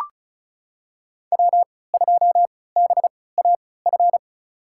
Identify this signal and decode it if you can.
Morse code, heard as a keyed tone, about 34 wpm, spaced as for 20 wpm, 700 Hz. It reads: W2BAF
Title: USA Call Signs - 5 Characters - 3x Character Spacing - 35wpm